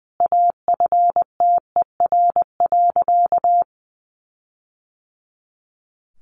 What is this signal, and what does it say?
Morse code, heard as a keyed tone, about 20 wpm, 700 Hz. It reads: AFTER.
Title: Sentences from Top 1500 Words 20wpm